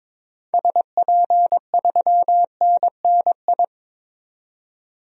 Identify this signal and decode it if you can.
Morse code, heard as a keyed tone, about 22 wpm, 700 Hz. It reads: SP3NNI
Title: DXCC-100 Call Signs - 6 Characters - 22wpm